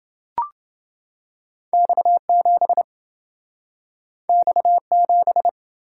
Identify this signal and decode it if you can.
Morse code, heard as a keyed tone, about 27 wpm, 700 Hz. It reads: X7 X7